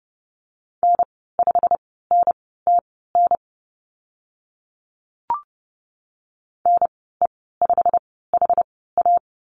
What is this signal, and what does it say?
Morse code, heard as a keyed tone, about 30 wpm, spaced as for 17 wpm, 700 Hz. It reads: N5NTN NE5HA
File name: Lesson 15g - Number 5 - Call Signs 30wpm